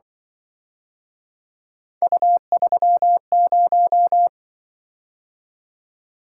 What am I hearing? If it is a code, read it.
Morse code, heard as a keyed tone, about 24 wpm, 700 Hz. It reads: U30
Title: Speed-Racing - Three Letter-Number Combinations - 20wpm